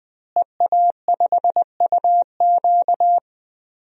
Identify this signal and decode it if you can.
Morse code, heard as a keyed tone, about 20 wpm, 700 Hz. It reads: EA5UQ
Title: DXCC-100 Call Signs - 5 Characters - Repeated 2x - 20wpm